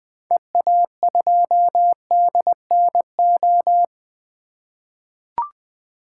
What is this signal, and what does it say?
Morse code, heard as a keyed tone, about 20 wpm, 700 Hz. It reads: EA2DNO
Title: 2023 SOTA Activator Call Signs - 20wpm